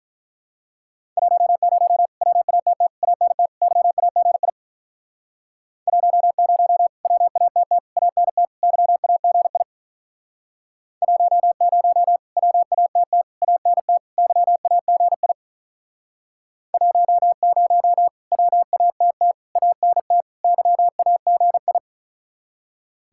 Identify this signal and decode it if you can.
Morse code, heard as a keyed tone, about 47 wpm, 700 Hz. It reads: 10 WATT ANT YAGI 10 WATT ANT YAGI 10 WATT ANT YAGI 10 WATT ANT YAGI